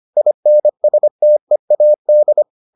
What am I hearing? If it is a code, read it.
Morse code, heard as a keyed tone, about 25 wpm, 600 Hz. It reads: INSTEAD